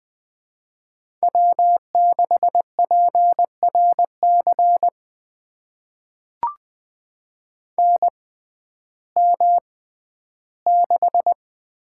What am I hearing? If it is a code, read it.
Morse code, heard as a keyed tone, about 20 wpm, spaced as for 12 wpm, 700 Hz. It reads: W6PRC NM6
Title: USA Call Signs - 6x Character Spacing - 20wpm